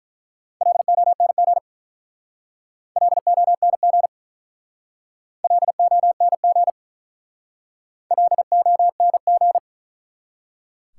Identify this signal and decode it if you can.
Morse code, heard as a keyed tone, about 42 wpm, 700 Hz. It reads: LONG LONG LONG LONG
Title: Speed-Racing Top 180 QSO Elements 35wpm